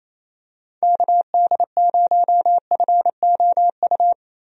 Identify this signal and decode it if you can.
Morse code, heard as a keyed tone, about 28 wpm, 700 Hz. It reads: KD0FOU
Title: USA Call Signs - 6 Characters - Repeated 2x - 28wpm